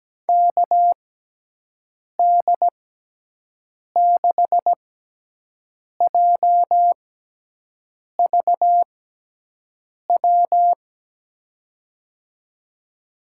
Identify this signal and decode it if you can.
Morse code, heard as a keyed tone, about 17 wpm, spaced as for 6 wpm, 700 Hz. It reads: KD6JVW
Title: USA Call Signs - 6 Characters - 6x Character Spacing - 17wpm